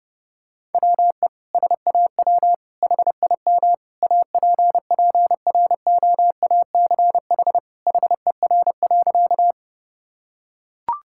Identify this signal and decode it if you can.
Morse code, heard as a keyed tone, about 32 wpm, 700 Hz. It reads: WE SAW HIM APPROACH HER.
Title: Sentences from Top 1000 Words - Review Part 6 - 30wpm